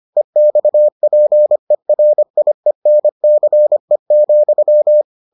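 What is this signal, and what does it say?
Morse code, heard as a keyed tone, about 25 wpm, 600 Hz. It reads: EXPERIENCE,
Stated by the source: Morse edition of the Foundations of Amateur Radio episodes.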